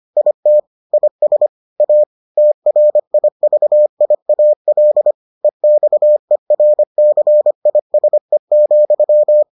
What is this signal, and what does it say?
Morse code, heard as a keyed tone, about 25 wpm, 600 Hz. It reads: IT IS A TRIVIAL EXERCISE,